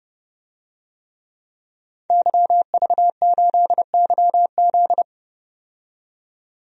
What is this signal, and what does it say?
Morse code, heard as a keyed tone, about 30 wpm, 700 Hz. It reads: YV8YZ
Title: DXCC-100 Call Signs - 5 Characters - 30wpm